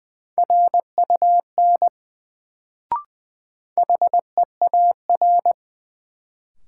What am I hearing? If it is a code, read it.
Morse code, heard as a keyed tone, about 20 wpm, 700 Hz. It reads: RUN HEAR
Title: Sentences from Top 300 Words - New Word Drill - 20wpm